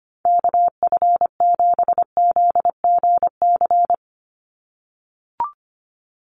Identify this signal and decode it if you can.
Morse code, heard as a keyed tone, about 25 wpm, 700 Hz. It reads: KF7ZGC
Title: USA Call Signs 25wpm